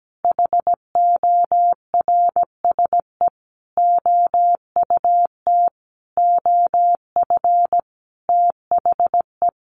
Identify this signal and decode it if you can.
Morse code, heard as a keyed tone, about 17 wpm, 700 Hz. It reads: HORSE OUT OF THE
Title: Sentences from Top 1500 Words 17wpm